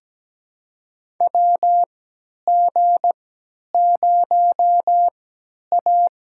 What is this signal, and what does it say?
Morse code, heard as a keyed tone, about 17 wpm, spaced as for 10 wpm, 700 Hz. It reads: WG0A
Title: USA Call Signs - 3x Character Spacing - 17wpm